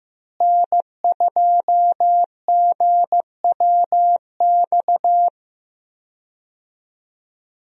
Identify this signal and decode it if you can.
Morse code, heard as a keyed tone, about 15 wpm, 700 Hz. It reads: N2GWX
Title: USA Call Signs - 5 Characters - Repeated 2x - 15wpm